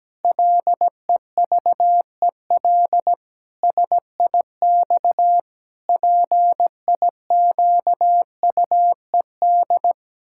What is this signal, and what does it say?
Morse code, heard as a keyed tone, about 17 wpm, 700 Hz. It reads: LEVEL SIX PIQUED
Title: Pangram Challenge v2 - 17wpm